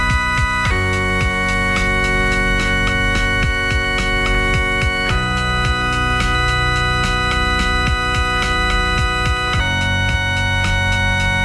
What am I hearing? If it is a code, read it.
Dial-up modem handshake.